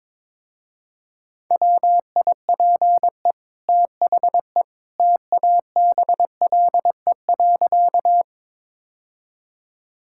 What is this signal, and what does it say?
Morse code, heard as a keyed tone, about 22 wpm, 700 Hz. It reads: WIPE THE TABLE.
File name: practice / 22wpm / Easy Sentences with Hints - 22wpm